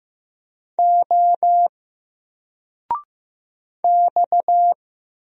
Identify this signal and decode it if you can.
Morse code, heard as a keyed tone, about 15 wpm, 700 Hz. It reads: O X
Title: Single Letter-Number 15wpm